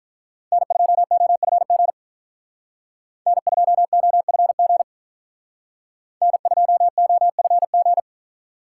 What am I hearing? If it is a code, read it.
Morse code, heard as a keyed tone, about 45 wpm, 700 Hz. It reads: N2OFG N2OFG N2OFG